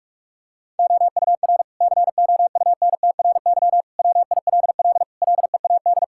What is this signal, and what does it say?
Morse code, heard as a keyed tone, about 47 wpm, 700 Hz. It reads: OUR COUNTRY WILL LEAD
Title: Sentences from Top 1000 Words - Review Part 5 - 45wpm